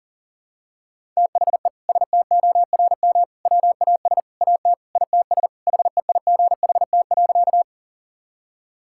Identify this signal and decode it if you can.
Morse code, heard as a keyed tone, about 40 wpm, 700 Hz. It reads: THE STORM WAS AT ITS HEIGHT.
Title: Sentences from Top 1600 Words 40wpm